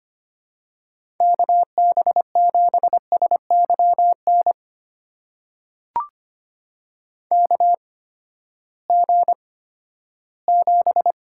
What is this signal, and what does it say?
Morse code, heard as a keyed tone, about 25 wpm, spaced as for 14 wpm, 700 Hz. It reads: KB7SYN KG7
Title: USA Call Signs - 6 Characters - 8x Character Spacing - 25wpm